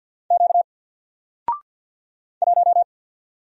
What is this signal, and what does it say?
Morse code, heard as a keyed tone, about 53 wpm, 700 Hz. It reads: Q 1
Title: Single Letter-Number Alt 50wpm